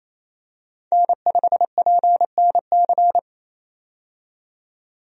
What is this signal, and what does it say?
Morse code, heard as a keyed tone, about 28 wpm, 700 Hz. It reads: N5PNC
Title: USA Call Signs - 5 Characters - 28wpm